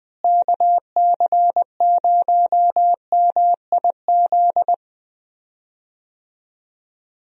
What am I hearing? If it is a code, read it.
Morse code, heard as a keyed tone, about 20 wpm, 700 Hz. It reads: KC0MIZ
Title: USA Call Signs - 6 Characters - Repeated 2x - 20wpm